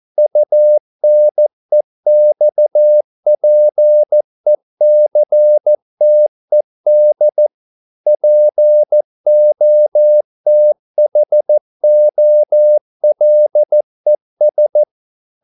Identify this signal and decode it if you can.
Morse code, heard as a keyed tone, about 14 wpm, 600 Hz. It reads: UNEXPECTED POTHOLES